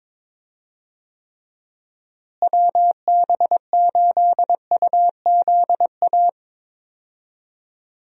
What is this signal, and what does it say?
Morse code, heard as a keyed tone, about 22 wpm, 700 Hz. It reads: WB8UZA
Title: USA Call Signs - 6 Characters - Repeated 2x - 22wpm